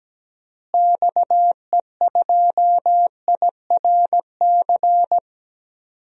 Morse code, heard as a keyed tone, about 17 wpm, 700 Hz. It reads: XE2IRC
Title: DXCC-100 Call Signs - 6 Characters - Repeated 3x - 17wpm